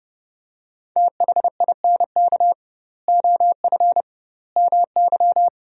Morse code, heard as a keyed tone, about 30 wpm, 700 Hz. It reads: THINK OF MY
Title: 2x Word-Spacing - Sentences from Top 300 Words - 30wpm